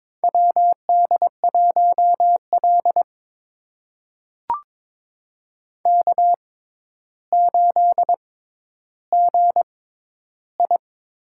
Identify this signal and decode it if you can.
Morse code, heard as a keyed tone, about 22 wpm, spaced as for 11 wpm, 700 Hz. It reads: WD1L K8GI
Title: USA Call Signs - 4 Characters - 6x Character Spacing - 22wpm